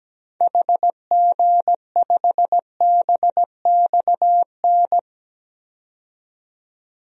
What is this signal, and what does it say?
Morse code, heard as a keyed tone, about 17 wpm, 700 Hz. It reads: HG5BXN
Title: DXCC-100 Call Signs - 6 Characters - 17wpm